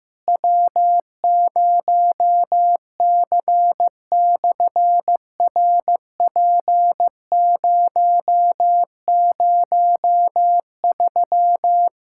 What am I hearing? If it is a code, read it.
Morse code, heard as a keyed tone, about 15 wpm, 700 Hz. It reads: W0C/RP003
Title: SOTA QSO Practice 15wpm